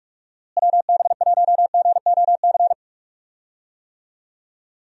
Morse code, heard as a keyed tone, about 46 wpm, 700 Hz. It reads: WB1GOC
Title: USA Call Signs - 6 Characters - Repeated 3x - 45wpm